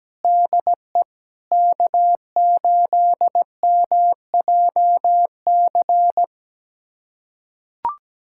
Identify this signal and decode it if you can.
Morse code, heard as a keyed tone, about 17 wpm, 700 Hz. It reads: DE K8MJC
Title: USA Callsign DE Callsign 17wpm